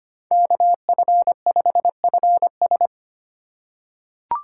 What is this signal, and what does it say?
Morse code, heard as a keyed tone, about 25 wpm, 700 Hz. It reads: KF5FS
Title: USA Call Signs - 3x Character Spacing - 25wpm